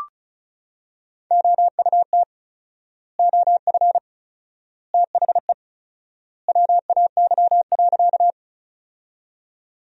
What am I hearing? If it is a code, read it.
Morse code, heard as a keyed tone, about 34 wpm, 700 Hz. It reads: OUT OF THE WAY.
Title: 4x Word-Spacing - Sentences from Top 100 Words - 35wpm